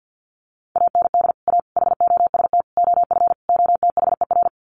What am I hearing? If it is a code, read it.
Morse code, heard as a keyed tone, about 49 wpm, 700 Hz. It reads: AND A HOST OF OTHER
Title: Sets of 5 Words Encore2 50wpm